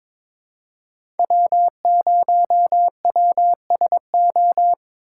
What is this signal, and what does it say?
Morse code, heard as a keyed tone, about 22 wpm, 700 Hz. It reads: W0WSO
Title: USA Call Signs - 5 Characters - 2x Character Spacing - 22wpm